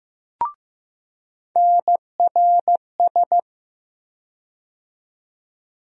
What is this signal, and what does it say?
Morse code, heard as a keyed tone, about 15 wpm, 700 Hz. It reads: NRS
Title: USA Call Sign Postfixes - 3 Letters 15wpm